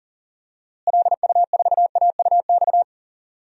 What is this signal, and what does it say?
Morse code, heard as a keyed tone, about 40 wpm, 700 Hz. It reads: LU4AUX
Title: DXCC-100 Call Signs - 6 Characters - Repeated 2x - 40wpm